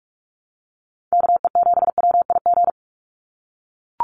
Morse code, heard as a keyed tone, about 44 wpm, 700 Hz. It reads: KE7WIG
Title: USA Call Signs 45wpm